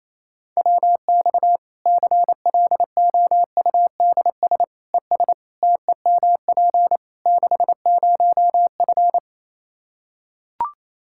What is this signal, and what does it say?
Morse code, heard as a keyed tone, about 28 wpm, 700 Hz. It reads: WX CLOUDS ES TEMP 60F
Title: Practice QSO - Exchange 2 - Info - 28wpm